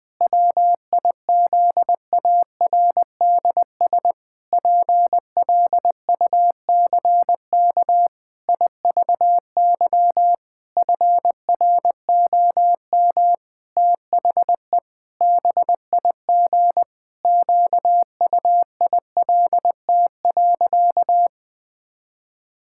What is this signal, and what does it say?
Morse code, heard as a keyed tone, about 19 wpm, 700 Hz. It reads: WIZARDS PLUCK IVY FROM THE BIG QUILT.
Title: Pangram Challenge v2 - 20wpm